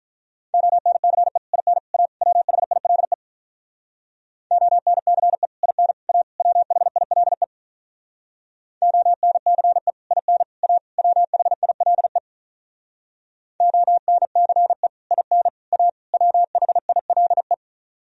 Morse code, heard as a keyed tone, about 46 wpm, 700 Hz. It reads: ONCE IN A WHILE ONCE IN A WHILE ONCE IN A WHILE ONCE IN A WHILE